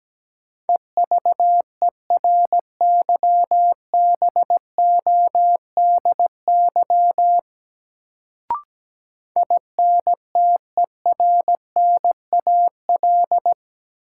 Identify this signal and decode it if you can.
Morse code, heard as a keyed tone, about 17 wpm, 700 Hz. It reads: EVERYBODY INTERNAL
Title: Sentences from Top 1400 Words - New Word Drill - 17wpm